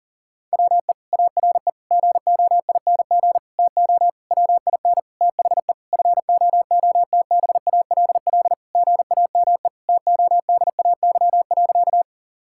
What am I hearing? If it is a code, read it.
Morse code, heard as a keyed tone, about 37 wpm, 700 Hz. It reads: WE ARE GOING TO WIN THE FOOTBALL GAME TODAY.
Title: Sentences from Top 1600 Words 40wpm